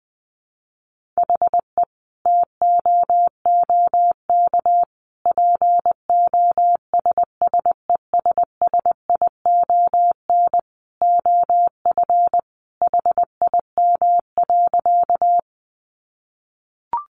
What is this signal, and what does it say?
Morse code, heard as a keyed tone, about 20 wpm, 700 Hz. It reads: HE TOOK POSSESSION OF HIM.
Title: Sentences from Top 1400 Words 20wpm